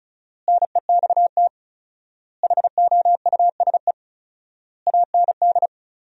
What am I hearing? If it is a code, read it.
Morse code, heard as a keyed tone, about 36 wpm, 700 Hz. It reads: NEXT HOUSE AND
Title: 4x Word-Spacing - Sentences from Top 400 Words - 35wpm